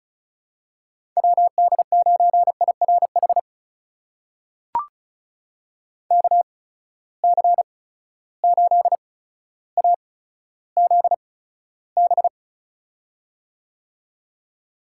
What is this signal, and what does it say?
Morse code, heard as a keyed tone, about 36 wpm, spaced as for 15 wpm, 700 Hz. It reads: WD9IRH KC8AZB
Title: USA Call Signs - 8x Character Spacing - 35wpm